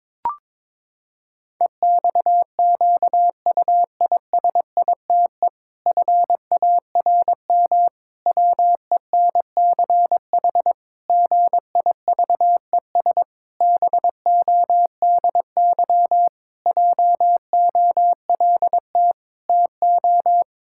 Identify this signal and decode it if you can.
Morse code, heard as a keyed tone, about 22 wpm, 700 Hz. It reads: EXQUISITE FARM WENCH GIVES BODY JOLT TO
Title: Pangram Challenge v2 - 22wpm